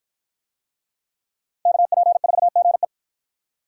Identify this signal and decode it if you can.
Morse code, heard as a keyed tone, about 53 wpm, 700 Hz. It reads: KP4GE